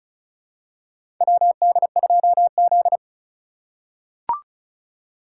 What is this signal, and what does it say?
Morse code, heard as a keyed tone, about 34 wpm, 700 Hz. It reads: WD2Z
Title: USA Call Signs - 2x Character Spacing - 35wpm